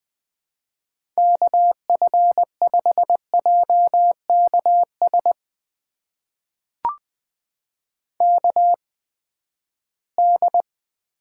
Morse code, heard as a keyed tone, about 20 wpm, spaced as for 14 wpm, 700 Hz. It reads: KF5JKS KD
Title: USA Call Signs - 6 Characters - 8x Character Spacing - 20wpm